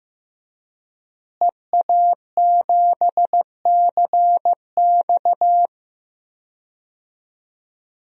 Morse code, heard as a keyed tone, about 15 wpm, 700 Hz. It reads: EA7CX
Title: DXCC-100 Call Signs - 5 Characters - Repeated 2x - 15wpm